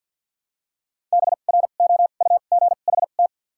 Morse code, heard as a keyed tone, about 53 wpm, 700 Hz. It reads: B R O U G H T
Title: 2x Character Spacing - Top 500 Words - 50wpm